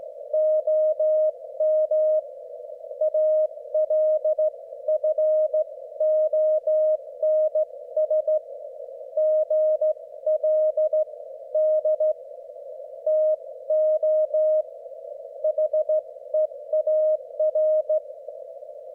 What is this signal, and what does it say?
Morse code, heard as a keyed tone, about 14 wpm, 620 Hz. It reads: OM ALFONS GLD TO HEAR
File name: Morsekurs / Witze / QSO W4 HB9DAL und HB9BPH